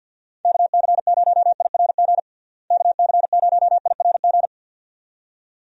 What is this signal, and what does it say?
Morse code, heard as a keyed tone, about 49 wpm, 700 Hz. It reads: KC0IRG KC0IRG
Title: USA Call Signs - 6 Characters - Repeated 2x - 50wpm